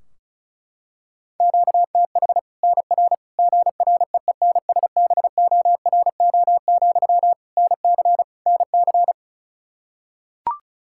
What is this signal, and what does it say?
Morse code, heard as a keyed tone, about 36 wpm, 700 Hz. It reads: QTH NR GREENSBORO, NC NC
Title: Practice QSO - Exchange 1 - Beginning - 35wpm